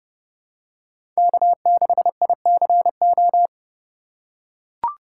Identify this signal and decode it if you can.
Morse code, heard as a keyed tone, about 30 wpm, 700 Hz. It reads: K6ICO
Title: USA Call Signs - 5 Characters - 4x Character Spacing - 30wpm